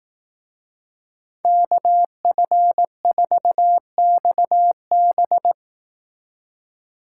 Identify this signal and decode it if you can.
Morse code, heard as a keyed tone, about 18 wpm, 700 Hz. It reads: KF4XB
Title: Speed Racing - USA 2x2 Call Signs 15wpm